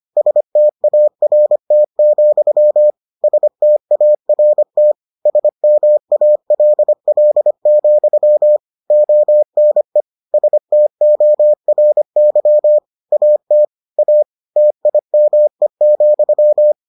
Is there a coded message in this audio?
Yes: Morse code from 0.2 to 16.7 s, heard as a keyed tone, about 25 wpm, 600 Hz. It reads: START, START SMALL, ONE STORY AT A TIME,